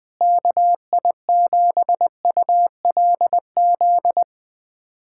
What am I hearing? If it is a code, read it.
Morse code, heard as a keyed tone, about 20 wpm, 700 Hz. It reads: KI7ULZ
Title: USA Callsign DE Callsign 20wpm